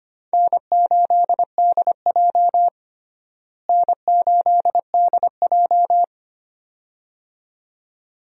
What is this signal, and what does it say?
Morse code, heard as a keyed tone, about 25 wpm, 700 Hz. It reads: N8DJ N8DJ
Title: USA Call Signs - 4 Characters - Repeated 2x - 25wpm